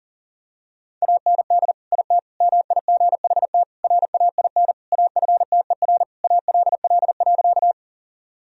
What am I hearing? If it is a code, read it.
Morse code, heard as a keyed tone, about 37 wpm, 700 Hz. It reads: AND IT MIGHT RAIN AFTER ALL.
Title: Sentences from Top 1000 Words - Review Part 1 - 40wpm